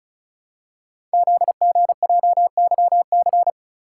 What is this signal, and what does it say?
Morse code, heard as a keyed tone, about 36 wpm, 700 Hz. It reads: ZGJYC
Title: Code Groups - 5 Letter:Numbers - 35wpm